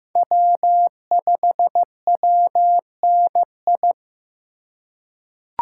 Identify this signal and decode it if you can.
Morse code, heard as a keyed tone, about 15 wpm, 700 Hz. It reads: W5WNI
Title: USA Call Signs - 5 Characters - 4x Character Spacing - 15wpm